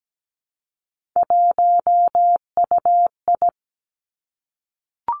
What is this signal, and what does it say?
Morse code, heard as a keyed tone, about 17 wpm, 700 Hz. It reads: 1UI
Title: Three Challenging Letter-Number Combinations 17wpm